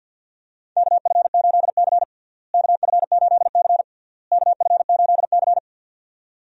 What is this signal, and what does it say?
Morse code, heard as a keyed tone, about 46 wpm, 700 Hz. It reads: KF8C KF8C KF8C
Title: USA Call Signs - 4 Characters - Repeated 3x - 50wpm